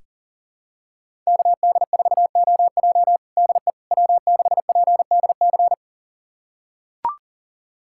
Morse code, heard as a keyed tone, about 40 wpm, 700 Hz. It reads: KD4OJ DE W6PDC
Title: USA Callsign DE Callsign - 5 Characters - 40wpm